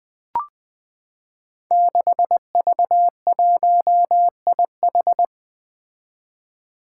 Morse code, heard as a keyed tone, about 20 wpm, 700 Hz. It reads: 6V1IH
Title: Five Challenging Letter-Number Combinations 20wpm